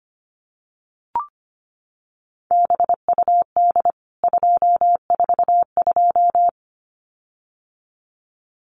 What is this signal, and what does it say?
Morse code, heard as a keyed tone, about 25 wpm, 700 Hz. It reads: BUD 242
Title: CWT Exchange 25wpm